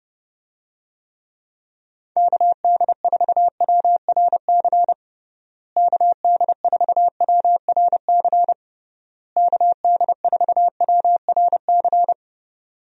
Morse code, heard as a keyed tone, about 32 wpm, 700 Hz. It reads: KD4WRC KD4WRC KD4WRC
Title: USA Call Signs - 6 Characters - Repeated 3x - 30wpm